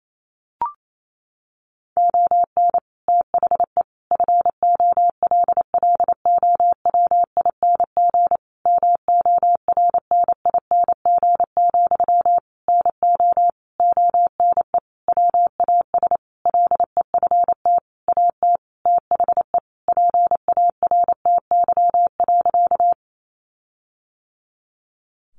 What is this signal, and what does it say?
Morse code, heard as a keyed tone, about 28 wpm, 700 Hz. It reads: ON THE FOLLOWING MORNING, NO ONE WAS LEFT AT THE PARTY.
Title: Sentences from Top 1400 Words 28wpm